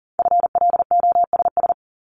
Morse code, heard as a keyed tone, about 38 wpm, 690 Hz. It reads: FLOSS
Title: Multi-letter Phonemes 40wpm